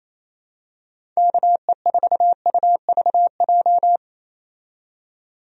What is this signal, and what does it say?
Morse code, heard as a keyed tone, about 28 wpm, 700 Hz. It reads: KE4UVJ
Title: USA Call Signs - 6 Characters - 28wpm